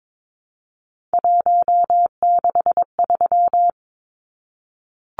Morse code, heard as a keyed tone, about 22 wpm, 700 Hz. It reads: 163
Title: Numbers - 3-Digit 22wpm